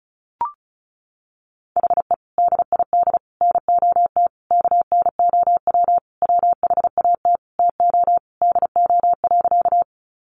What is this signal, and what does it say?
Morse code, heard as a keyed tone, about 34 wpm, 700 Hz. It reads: HE DID NOT KNOW WHAT TO DO.
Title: Sentences from Top 600 Words 35wpm